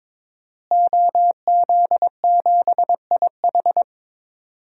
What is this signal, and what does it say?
Morse code, heard as a keyed tone, about 22 wpm, 700 Hz. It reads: OZ7IH
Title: DXCC-100 Call Signs - 5 Characters - Repeated 2x - 22wpm